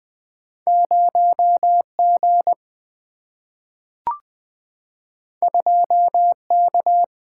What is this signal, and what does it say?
Morse code, heard as a keyed tone, about 20 wpm, 700 Hz. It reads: 0 G 2 K
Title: Single-Letter Single-Number 20wpm